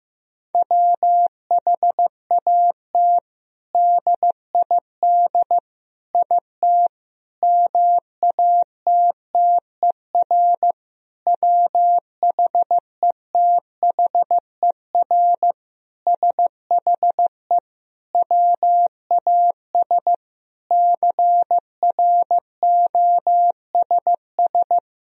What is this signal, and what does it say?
Morse code, heard as a keyed tone, about 15 wpm, 700 Hz. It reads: WHAT DID IT MATTER WHETHER SHE WAS CROSS
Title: Sentences from Top 1000 Words - Review Part 3 - 15wpm